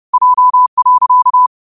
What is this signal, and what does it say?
Morse code, heard as a keyed tone, about 30 wpm, 1000 Hz. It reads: J.
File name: qotdfiles / 30wpm